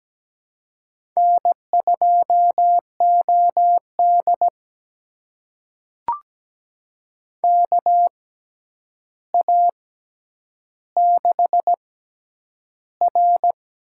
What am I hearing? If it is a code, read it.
Morse code, heard as a keyed tone, about 17 wpm, spaced as for 9 wpm, 700 Hz. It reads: N2OD KA6R
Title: USA Call Signs - 6x Character Spacing - 17wpm